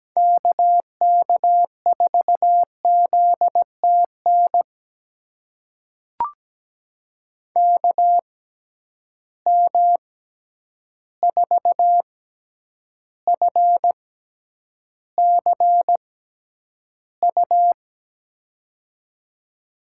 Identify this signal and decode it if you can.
Morse code, heard as a keyed tone, about 17 wpm, spaced as for 9 wpm, 700 Hz. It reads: KK4ZTN KM4FCU
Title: USA Call Signs - 6 Characters - 6x Character Spacing - 17wpm